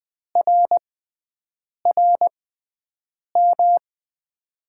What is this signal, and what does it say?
Morse code, heard as a keyed tone, about 20 wpm, spaced as for 7 wpm, 700 Hz. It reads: RRM
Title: USA Call Signs - 6 Characters - 6x Character Spacing - 20wpm